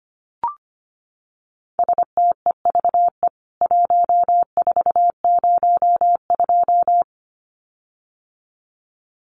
Morse code, heard as a keyed tone, about 25 wpm, 700 Hz. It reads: STEVE 1402
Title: CWT Exchange 25wpm